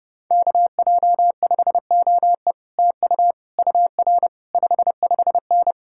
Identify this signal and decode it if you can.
Morse code, heard as a keyed tone, about 33 wpm, 700 Hz. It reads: KJ5OE TU UR 55N